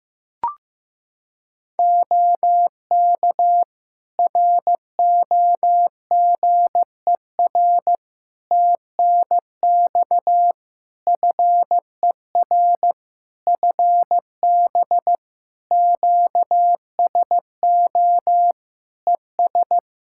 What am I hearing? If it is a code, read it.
Morse code, heard as a keyed tone, about 15 wpm, 700 Hz. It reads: OK ROGER TNX FER FB QSO ES
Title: Practice QSO - Exchange 4 - Ending - 15wpm